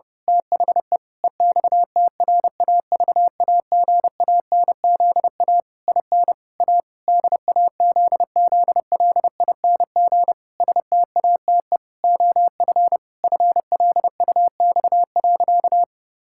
Morse code, heard as a keyed tone, about 32 wpm, 700 Hz. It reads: THE EXTRAVAGANZA IN A DAZZLING STATE OF FLUX.
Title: Pangram Challenge v2 - 30wpm